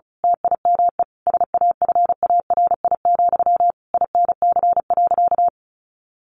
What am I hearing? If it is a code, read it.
Morse code, heard as a keyed tone, about 34 wpm, 700 Hz. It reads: TIME SAFARI, INC.
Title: A Sound Of Thunder 35wpm